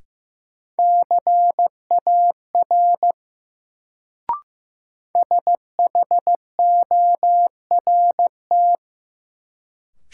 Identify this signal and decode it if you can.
Morse code, heard as a keyed tone, about 16 wpm, 700 Hz. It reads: CAR SHORT
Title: Sets of 5 Words Encore - New Word Drill - 15wpm